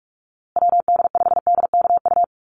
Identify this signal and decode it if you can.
Morse code, heard as a keyed tone, about 44 wpm, 700 Hz. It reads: WD5DKU
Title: USA Call Signs Repeated 3x 45wpm